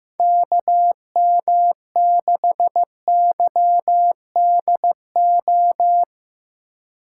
Morse code, heard as a keyed tone, about 15 wpm, 700 Hz. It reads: KM6YDO
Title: USA Call Signs - 6 Characters - 4x Character Spacing - 15wpm